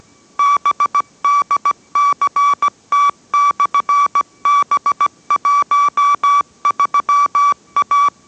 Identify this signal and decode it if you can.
Morse code, heard as a keyed tone, about 17 wpm, 1150 Hz. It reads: BDCT/B13A